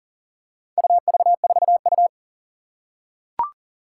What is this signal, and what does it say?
Morse code, heard as a keyed tone, about 42 wpm, 700 Hz. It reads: UV4U
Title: Four Challenging Letter-Number Combinations 40wpm